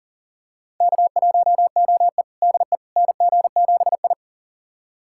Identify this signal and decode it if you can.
Morse code, heard as a keyed tone, about 37 wpm, 700 Hz. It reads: K1OE DE NG7I
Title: USA Callsign DE Callsign - 4 Characters - 40wpm